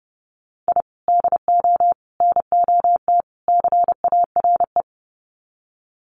Morse code, heard as a keyed tone, about 29 wpm, 700 Hz. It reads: I DO NOT CARE
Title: Sets of 4 Words 30wpm